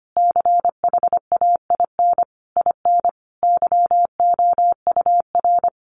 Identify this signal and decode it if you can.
Morse code, heard as a keyed tone, about 25 wpm, 700 Hz. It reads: CHAIN IN YOUR
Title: Sentences from Top 1500 Words 25wpm